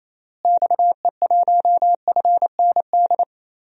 Morse code, heard as a keyed tone, about 28 wpm, 700 Hz. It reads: XE1FND
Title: DXCC-100 Call Signs - 6 Characters - Repeated 3x - 28wpm